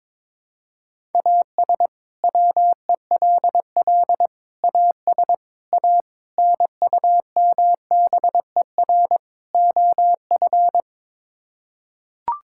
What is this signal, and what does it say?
Morse code, heard as a keyed tone, about 22 wpm, 700 Hz. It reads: AS WELL AS A NUMBER OF
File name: practice / 22wpm / Sets of 6 Words - 22wpm